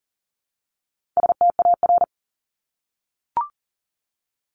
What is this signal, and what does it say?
Morse code, heard as a keyed tone, about 40 wpm, 700 Hz. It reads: STAR
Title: Top 500 Words 40wpm